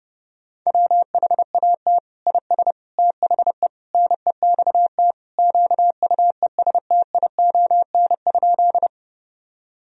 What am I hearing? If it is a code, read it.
Morse code, heard as a keyed tone, about 32 wpm, 700 Hz. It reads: WHAT IS THE NEXT QUESTION?
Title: Sentences from Top 1000 Words - Review Part 1 - 30wpm